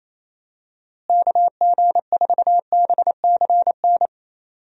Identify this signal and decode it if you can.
Morse code, heard as a keyed tone, about 28 wpm, 700 Hz. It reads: KG4BCN
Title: USA Call Signs - 6 Characters - Repeated 3x - 28wpm